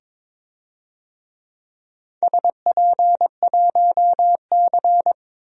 Morse code, heard as a keyed tone, about 22 wpm, 700 Hz. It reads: SP1C
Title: DXCC-100 Call Signs - 4 Characters - 22wpm